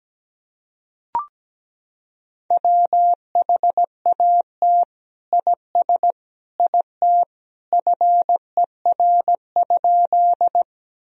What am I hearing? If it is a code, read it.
Morse code, heard as a keyed tone, about 17 wpm, 700 Hz. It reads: WHAT IS IT FER?
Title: Sentences from Top 100 Words and Abbreviations 17wpm